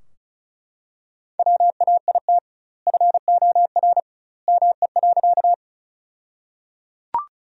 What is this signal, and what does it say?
Morse code, heard as a keyed tone, about 34 wpm, 700 Hz. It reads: WAIT FOR ME.
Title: 2x Word-Spacing - Sentences from Top 500 Words - 35wpm